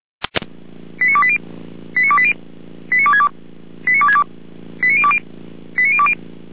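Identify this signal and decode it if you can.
Five-tone selective call (ZVEI), heard as quick runs of five tones, tones of about 70 ms. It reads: 89280 89280 89262 89262 89020 89020
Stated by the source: SDR-received ZVEI five-tone selective call, code 89333.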